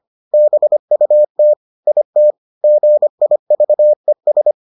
Morse code, heard as a keyed tone, about 25 wpm, 600 Hz. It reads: BUT IT GIVES